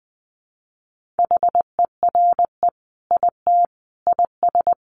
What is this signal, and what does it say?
Morse code, heard as a keyed tone, about 20 wpm, 700 Hz. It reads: HERE IT IS
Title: Sentences from Top 1500 Words 20wpm